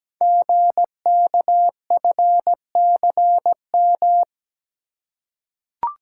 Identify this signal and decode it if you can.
Morse code, heard as a keyed tone, about 17 wpm, 700 Hz. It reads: GKFCM
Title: Code Groups - 5 Letters - 17wpm